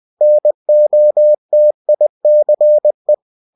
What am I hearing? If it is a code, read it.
Morse code, heard as a keyed tone, about 20 wpm, 600 Hz. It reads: NOTICE
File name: cw-books / winnie-the-pooh / winnie-20wpm-page-0031